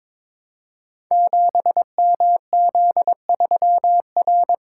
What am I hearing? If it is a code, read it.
Morse code, heard as a keyed tone, about 22 wpm, 700 Hz. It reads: 7MZ3R
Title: Code Groups - 5 Letter:Numbers - 22wpm